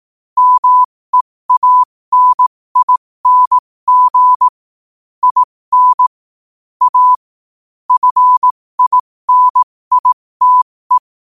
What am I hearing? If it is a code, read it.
Morse code, heard as a keyed tone, about 18 wpm, spaced as for 15 wpm, 1000 Hz. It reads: MEANING IN A FINITE
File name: qotdfiles / 15wpm